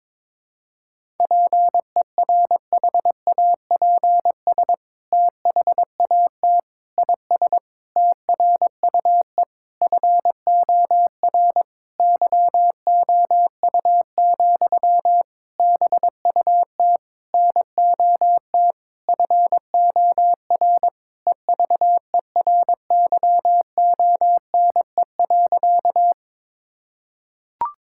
Morse code, heard as a keyed tone, about 22 wpm, 700 Hz. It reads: PERHAPS THAT IS TRUE FOR YOU, BUT NOT FOR EVERYONE.
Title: Sentences from Top 1000 Words - Review Part 5 - 22wpm